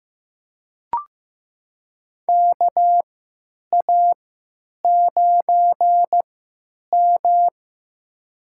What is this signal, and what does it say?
Morse code, heard as a keyed tone, about 15 wpm, spaced as for 8 wpm, 700 Hz. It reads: KA9M
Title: USA Call Signs - 4 Characters - 3x Character Spacing - 15wpm